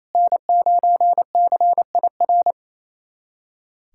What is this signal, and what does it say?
Morse code, heard as a keyed tone, about 27 wpm, 700 Hz. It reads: N9CIR
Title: USA Callsign DE Callsign - 5 Characters - 28wpm